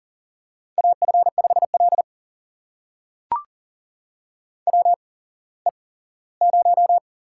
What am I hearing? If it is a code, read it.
Morse code, heard as a keyed tone, about 37 wpm, spaced as for 19 wpm, 700 Hz. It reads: AF5L WE0
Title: USA Call Signs - 4 Characters - 8x Character Spacing - 40wpm